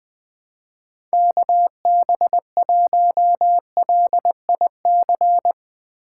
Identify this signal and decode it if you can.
Morse code, heard as a keyed tone, about 20 wpm, 700 Hz. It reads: KB1LIC
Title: Speed Racing - USA 2x3 Call Signs 17wpm